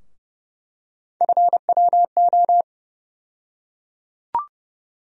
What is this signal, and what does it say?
Morse code, heard as a keyed tone, about 31 wpm, 700 Hz. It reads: FWO
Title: USA Call Sign Postfixes - 3 Letters 30wpm